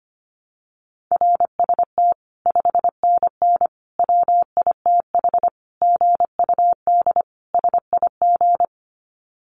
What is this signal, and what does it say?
Morse code, heard as a keyed tone, about 25 wpm, 700 Hz. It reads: RST 5NN WITH GUD SIG
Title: Common QSO Phrases 25wpm